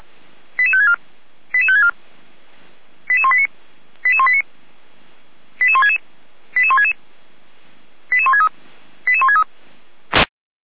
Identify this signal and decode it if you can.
Five-tone selective call (ZVEI), heard as quick runs of five tones, tones of about 80 ms. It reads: 88564 88564 88189 88189 88166 88166 88163 88163